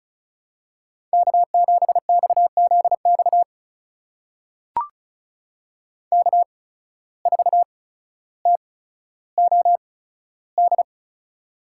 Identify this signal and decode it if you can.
Morse code, heard as a keyed tone, about 36 wpm, spaced as for 15 wpm, 700 Hz. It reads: K7XZX K4TOD
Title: USA Call Signs - 5 Characters - 8x Character Spacing - 35wpm